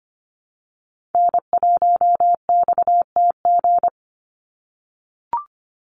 Morse code, heard as a keyed tone, about 25 wpm, 690 Hz. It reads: N1XTG
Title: USA Call Signs 25wpm